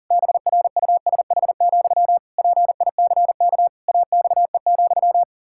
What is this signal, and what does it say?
Morse code, heard as a keyed tone, about 40 wpm, 700 Hz. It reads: BRUSH, PICK AXE,
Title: Pangram Challenge v2 - 40wpm